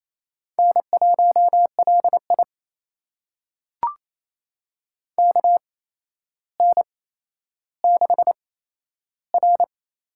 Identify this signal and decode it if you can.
Morse code, heard as a keyed tone, about 28 wpm, spaced as for 12 wpm, 700 Hz. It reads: N1LI KN6R
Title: USA Call Signs - 4 Characters - 8x Character Spacing - 28wpm